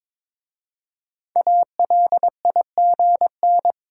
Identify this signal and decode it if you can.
Morse code, heard as a keyed tone, about 23 wpm, 700 Hz. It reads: ALIGN
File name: 5 Letter Words - First 3 Letters Given - 22wpm